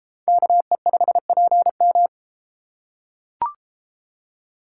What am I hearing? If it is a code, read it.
Morse code, heard as a keyed tone, about 31 wpm, 700 Hz. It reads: KE5PM